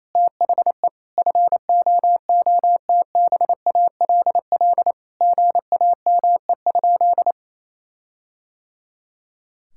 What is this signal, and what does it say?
Morse code, heard as a keyed tone, about 28 wpm, 700 Hz. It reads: THE FOOTBALL GAME?
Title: Sentences from Top 1600 Words 28wpm